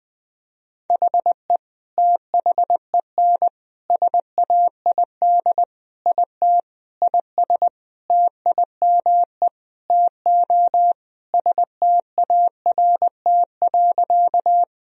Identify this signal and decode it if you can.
Morse code, heard as a keyed tone, about 21 wpm, 700 Hz. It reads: HE THEN SAID IT IS TIME TO START.
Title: Sentences from Top 1000 Words - Review Part 1 - 20wpm